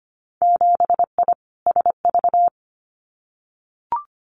Morse code, heard as a keyed tone, about 25 wpm, 700 Hz. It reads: 7I SV
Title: Winter Field Day Exchanges Encore 25wpm